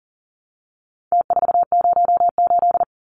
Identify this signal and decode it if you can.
Morse code, heard as a keyed tone, about 40 wpm, 700 Hz. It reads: T408
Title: Numbers - Contest Style - 4-Digit 40wpm